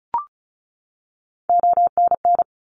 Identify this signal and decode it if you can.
Morse code, heard as a keyed tone, about 34 wpm, 700 Hz. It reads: ONN
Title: ARRL Sections 35wpm